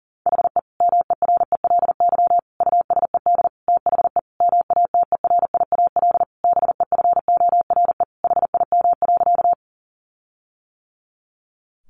Morse code, heard as a keyed tone, about 40 wpm, 700 Hz. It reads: HE MERELY USED THE MATERIAL BEFORE HIM.
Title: Sentences from Top 1400 Words 40wpm